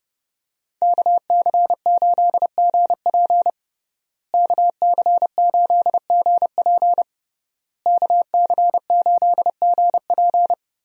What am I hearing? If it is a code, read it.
Morse code, heard as a keyed tone, about 32 wpm, 700 Hz. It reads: KC8GP KC8GP KC8GP
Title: USA Call Signs - 5 Characters - Repeated 3x - 30wpm